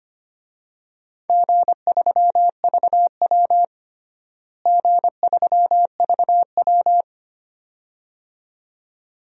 Morse code, heard as a keyed tone, about 25 wpm, 700 Hz. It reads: G3VW G3VW
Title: DXCC-100 Call Signs - 4 Characters - Repeated 2x - 25wpm